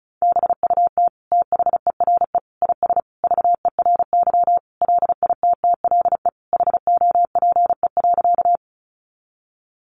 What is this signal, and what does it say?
Morse code, heard as a keyed tone, about 34 wpm, 700 Hz. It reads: BUT THERE IS VERY LITTLE HOPE.
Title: Sentences from Top 1000 Words 35wpm